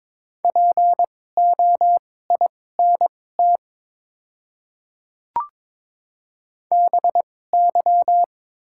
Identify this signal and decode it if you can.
Morse code, heard as a keyed tone, about 22 wpm, spaced as for 16 wpm, 700 Hz. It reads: POINT BY
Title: 2x Character Spacing - Top 300 Words - 22wpm